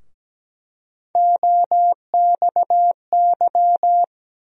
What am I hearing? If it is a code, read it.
Morse code, heard as a keyed tone, about 17 wpm, 700 Hz. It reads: OXY
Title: USA Call Sign Postfixes - 3 Letters 17wpm